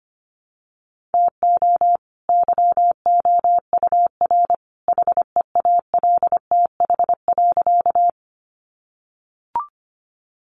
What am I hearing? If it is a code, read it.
Morse code, heard as a keyed tone, about 25 wpm, 700 Hz. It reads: TO YOUR HEALTH.
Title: Sentences from Top 600 Words 25wpm